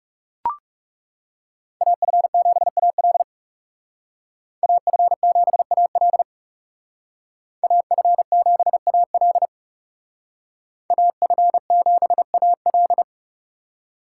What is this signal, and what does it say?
Morse code, heard as a keyed tone, about 38 wpm, 700 Hz. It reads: AF7AL AF7AL AF7AL AF7AL